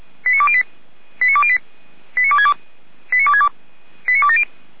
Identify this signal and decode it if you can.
Five-tone selective call (ZVEI), heard as quick runs of five tones, tones of about 70 ms. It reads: 89208 89208 89262 89262 89270